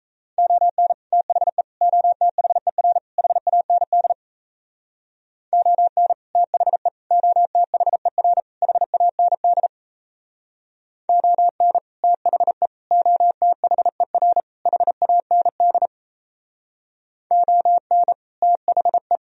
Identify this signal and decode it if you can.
Morse code, heard as a keyed tone, about 35 wpm, 700 Hz. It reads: ON THE OTHER HAND ON THE OTHER HAND ON THE OTHER HAND ON THE